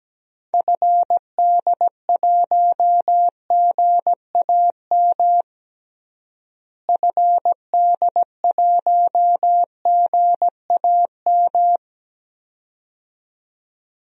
Morse code, heard as a keyed tone, about 17 wpm, 700 Hz. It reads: FD1GAM FD1GAM
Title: DXCC-100 Call Signs - 6 Characters - Repeated 2x - 17wpm